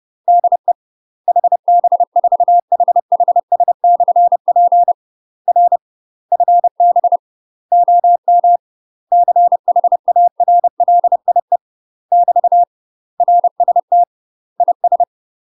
Morse code, heard as a keyed tone, about 30 wpm, 700 Hz. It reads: DE HB4HHS/P R FB OM CHARLIE <BT> RST IS
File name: UK MASTER QSOs TRAINING / 30WPM / UK MASTER QSOs 0038